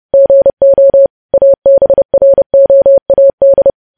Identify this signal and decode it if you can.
Morse code, heard as a keyed tone, about 30 wpm, 550 Hz. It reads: GO ABROAD
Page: news headlines in morse code at 30 wpm